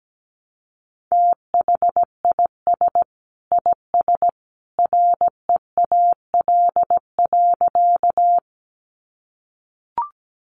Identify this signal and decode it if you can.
Morse code, heard as a keyed tone, about 17 wpm, 700 Hz. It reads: THIS IS REAL.
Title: Sentences from Top 300 Words 17wpm